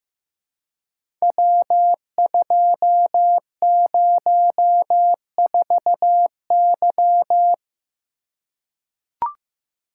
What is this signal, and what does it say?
Morse code, heard as a keyed tone, about 15 wpm, 700 Hz. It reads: W204Y
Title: Code Groups - 5 Letter:Numbers - 15wpm